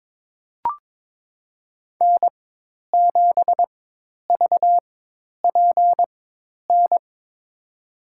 Morse code, heard as a keyed tone, about 22 wpm, spaced as for 10 wpm, 700 Hz. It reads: N7VPN
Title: USA Call Signs - 5 Characters - 4x Character Spacing - 22wpm